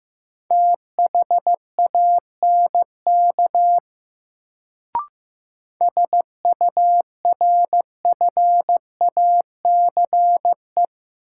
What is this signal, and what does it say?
Morse code, heard as a keyed tone, about 15 wpm, 700 Hz. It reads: THANK SURFACE
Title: Sentences from Top 600 Words - New Word Drill - 15wpm